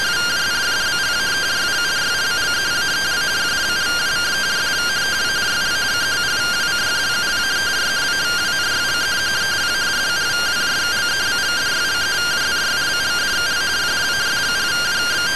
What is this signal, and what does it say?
Two-tone teleprinter signal, about 170 Hz shift.